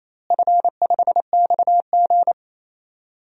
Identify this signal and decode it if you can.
Morse code, heard as a keyed tone, about 28 wpm, 700 Hz. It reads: F5XG
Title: DXCC-100 Call Signs - 4 Characters - Repeated 3x - 28wpm